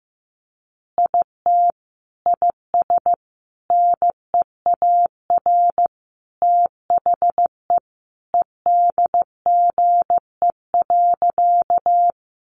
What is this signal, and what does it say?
Morse code, heard as a keyed tone, about 15 wpm, 700 Hz. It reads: IT IS NEAR THE EDGE.
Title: Additional Easy Phrases 15wpm